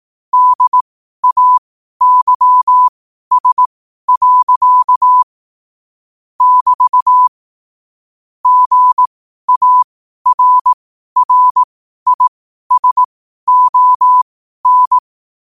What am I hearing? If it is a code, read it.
Morse code, heard as a keyed tone, about 18 wpm, spaced as for 12 wpm, 1000 Hz. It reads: DAYS. <BT> GARRISON
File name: qotdfiles / 12wpm